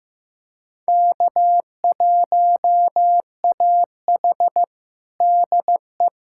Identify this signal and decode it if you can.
Morse code, heard as a keyed tone, about 16 wpm, 700 Hz. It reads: K1AH DE
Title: USA Callsign DE Callsign - 4 Characters - 15wpm